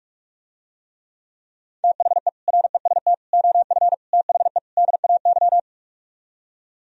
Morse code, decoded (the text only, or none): THE REST OF THE DAY